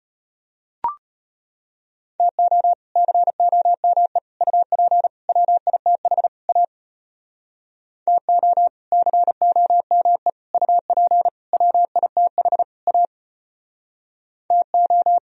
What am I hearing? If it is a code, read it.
Morse code, heard as a keyed tone, about 32 wpm, 700 Hz. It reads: TO COME UP WITH A TO COME UP WITH A TO